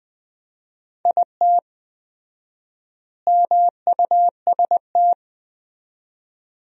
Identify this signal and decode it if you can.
Morse code, heard as a keyed tone, about 20 wpm, 700 Hz. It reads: IT MUST
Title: 4x Word-Spacing - Sentences from Top 500 Words - 20wpm